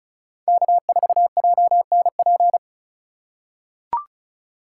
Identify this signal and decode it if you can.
Morse code, heard as a keyed tone, about 36 wpm, 700 Hz. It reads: K4JNP
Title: USA Call Signs - 5 Characters - 2x Character Spacing - 35wpm